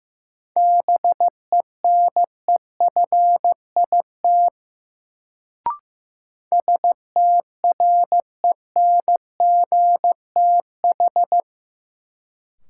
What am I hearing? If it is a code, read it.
Morse code, heard as a keyed tone, about 15 wpm, 700 Hz. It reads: BENEFIT STRENGTH
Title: Sentences from Top 1300 Words - New Word Drill Alt - 15wpm